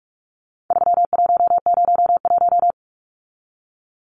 Morse code, heard as a keyed tone, about 44 wpm, 700 Hz. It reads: 3101
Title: Numbers - 4-Digit 45wpm